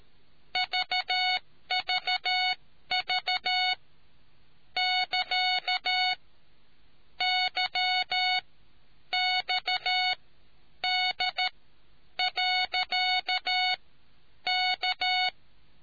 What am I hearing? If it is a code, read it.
Morse code, heard as a keyed tone, about 13 wpm, spaced as for 9 wpm, 740 Hz. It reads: VVV <KA> YXD.K